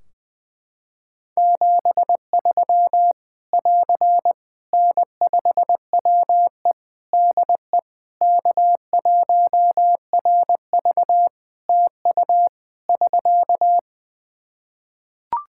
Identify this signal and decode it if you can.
Morse code, heard as a keyed tone, about 20 wpm, 700 Hz. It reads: 73 <AR> N5WE DE K1RV TU <SK>
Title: Practice QSO - Exchange 4 - Ending - 20wpm